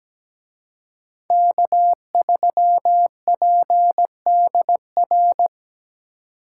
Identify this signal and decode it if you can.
Morse code, heard as a keyed tone, about 20 wpm, 700 Hz. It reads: K3PDR
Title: Speed Racing - USA 1x3 Call Signs 17wpm